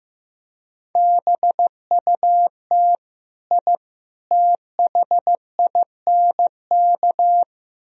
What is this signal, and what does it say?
Morse code, heard as a keyed tone, about 16 wpm, 700 Hz. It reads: BUT I THINK
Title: Speed-Racing - Sets of 3 Words 15wpm